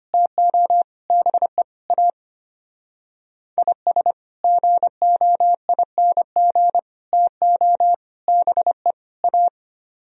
Morse code, heard as a keyed tone, about 30 wpm, 700 Hz. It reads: TO BE A IS GOING TO BE A
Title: Speed-Racing - Sets of 5 Words 25wpm